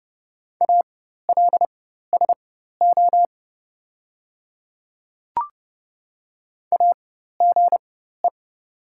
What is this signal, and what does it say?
Morse code, heard as a keyed tone, about 31 wpm, spaced as for 14 wpm, 700 Hz. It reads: ALSO AGE
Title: 4x Character Spacing - Top 500 Words - 30wpm